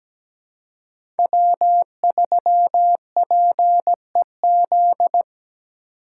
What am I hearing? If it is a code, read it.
Morse code, heard as a keyed tone, about 17 wpm, 700 Hz. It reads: W3PEZ
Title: USA Call Signs - 4x Character Spacing - 17wpm